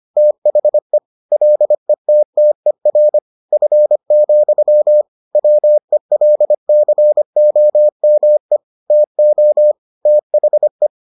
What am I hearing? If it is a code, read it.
Morse code, heard as a keyed tone, about 25 wpm, 600 Hz. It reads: THE LETTER F, WELCOME TO THE